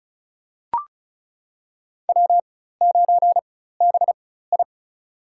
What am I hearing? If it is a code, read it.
Morse code, heard as a keyed tone, about 34 wpm, spaced as for 16 wpm, 700 Hz. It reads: W9BI
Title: USA Call Signs - 4x Character Spacing - 35wpm